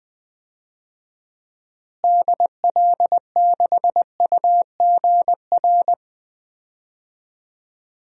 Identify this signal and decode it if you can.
Morse code, heard as a keyed tone, about 20 wpm, 700 Hz. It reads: DL6UGR
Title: DXCC-100 Call Signs - 6 Characters - Repeated 3x - 20wpm